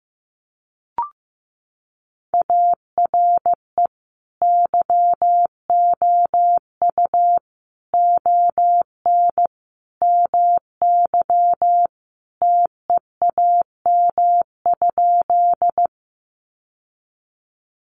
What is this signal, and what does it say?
Morse code, heard as a keyed tone, about 15 wpm, 700 Hz. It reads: ARE YOU ON MY TEAM?
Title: Sentences from Top 1500 Words 15wpm